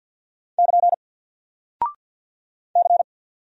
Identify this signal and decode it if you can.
Morse code, heard as a keyed tone, about 49 wpm, 700 Hz. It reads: <KN> C
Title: Single Character (Letter:Number:Punctuation:Prosign) Alt 50wpm